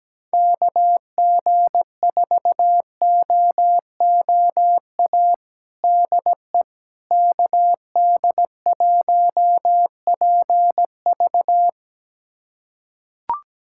Morse code, heard as a keyed tone, about 17 wpm, 700 Hz. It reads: KG4OOA DE KD1PV
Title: USA Callsign DE Callsign 17wpm